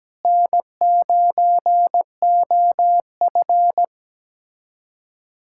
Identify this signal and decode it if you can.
Morse code, heard as a keyed tone, about 17 wpm, 700 Hz. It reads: N9OF
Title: USA Callsign DE Callsign - 4 Characters - 17wpm